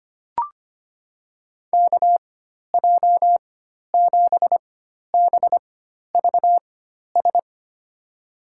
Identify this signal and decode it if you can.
Morse code, heard as a keyed tone, about 25 wpm, spaced as for 12 wpm, 700 Hz. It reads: KJ7BVS
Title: USA Call Signs - 6 Characters - 4x Character Spacing - 25wpm